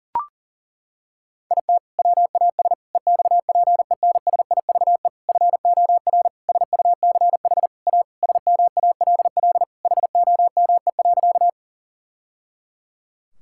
Morse code, heard as a keyed tone, about 40 wpm, 700 Hz. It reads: IT WAS EXPENSIVE FOR SUCH A SMALL HOME.
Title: Sentences from Top 1000 Words - Review Part 6 - 40wpm